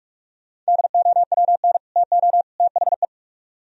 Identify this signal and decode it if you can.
Morse code, heard as a keyed tone, about 53 wpm, 700 Hz. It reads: DOWN TO THE
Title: Speed-Racing - Sets of 3 Words 45wpm